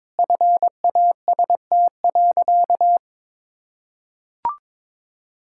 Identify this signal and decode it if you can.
Morse code, heard as a keyed tone, about 22 wpm, 700 Hz. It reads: FAST.
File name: practice / 22wpm / Sentences with 2-Word Phrases - 22wpm